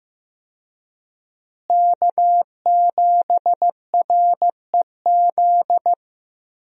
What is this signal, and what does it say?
Morse code, heard as a keyed tone, about 15 wpm, 700 Hz. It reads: K7REZ
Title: USA Call Signs - 5 Characters - Repeated 2x - 15wpm